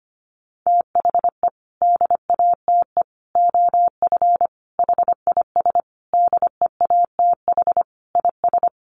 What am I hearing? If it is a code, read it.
Morse code, heard as a keyed tone, about 25 wpm, 700 Hz. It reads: THE DATE OF HIS DEATH IS
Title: Sentences from Top 1100 Words 25wpm